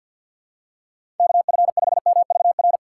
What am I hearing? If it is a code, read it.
Morse code, heard as a keyed tone, about 51 wpm, 700 Hz. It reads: KF5MVR
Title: USA Call Signs - 6 Characters - 6x Character Spacing - 50wpm